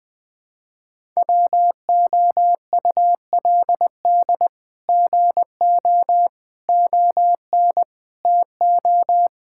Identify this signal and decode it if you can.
Morse code, heard as a keyed tone, about 20 wpm, 700 Hz. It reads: WOULD GO ON TO
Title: Sets of 8 Words - 20wpm